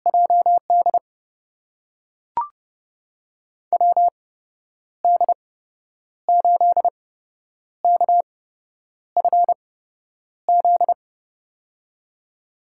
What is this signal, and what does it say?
Morse code, heard as a keyed tone, about 30 wpm, spaced as for 9 wpm, 700 Hz. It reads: JD WD8KFZ
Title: USA Call Signs - 6 Characters - 8x Character Spacing - 30wpm